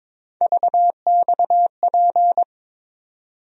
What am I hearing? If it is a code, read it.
Morse code, heard as a keyed tone, about 22 wpm, 700 Hz. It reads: VXP
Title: USA Call Sign Postfixes - 3 Letters 22wpm